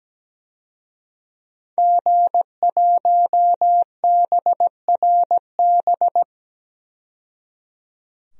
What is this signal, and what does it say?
Morse code, heard as a keyed tone, about 17 wpm, 700 Hz. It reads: G1BRB
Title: DXCC-100 Call Signs - 5 Characters - 17wpm